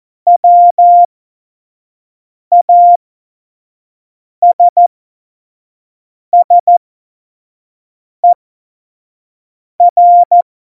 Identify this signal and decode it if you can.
Morse code, heard as a keyed tone, about 14 wpm, spaced as for 5 wpm, 700 Hz. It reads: WASSER
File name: Morsekurs / Witze / Witz 05. September 17, Tempo 25